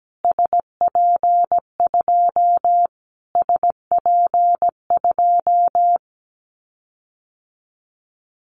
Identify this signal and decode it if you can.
Morse code, heard as a keyed tone, about 17 wpm, 700 Hz. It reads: SP2 SP2
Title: Callsign Prefixes for Top 25 DXCC - Repeated 3x - 17wpm